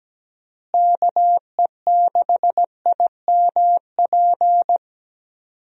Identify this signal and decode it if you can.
Morse code, heard as a keyed tone, about 17 wpm, 700 Hz. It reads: KE6IMP
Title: USA Call Signs - 6 Characters - Repeated 3x - 17wpm